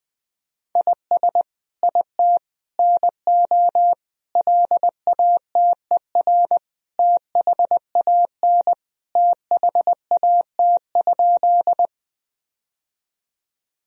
Morse code, heard as a keyed tone, about 20 wpm, 700 Hz. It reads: IS IT NO LATER THAN THAT?
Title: Sentences from Top 1000 Words - Review Part 4 - 20wpm